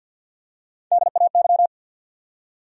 Morse code, heard as a keyed tone, about 49 wpm, 700 Hz. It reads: DAY
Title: Top 100 Words Repeated 50wpm